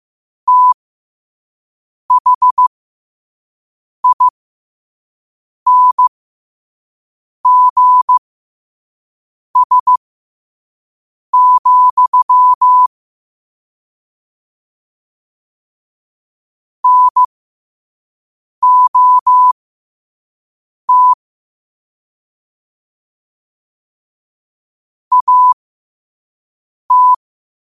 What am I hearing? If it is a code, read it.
Morse code, heard as a keyed tone, about 15 wpm, spaced as for 5 wpm, 1000 Hz. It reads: THINGS, NOT AT
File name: qotdfiles / 5wpm